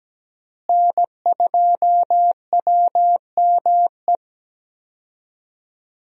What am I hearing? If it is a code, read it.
Morse code, heard as a keyed tone, about 17 wpm, 700 Hz. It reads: N2WME
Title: USA Call Signs - 5 Characters - Repeated 2x - 17wpm